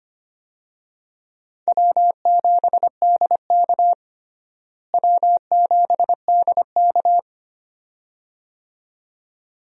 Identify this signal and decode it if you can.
Morse code, heard as a keyed tone, about 25 wpm, 700 Hz. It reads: W7DK W7DK
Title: USA Call Signs - 4 Characters - Repeated 2x - 25wpm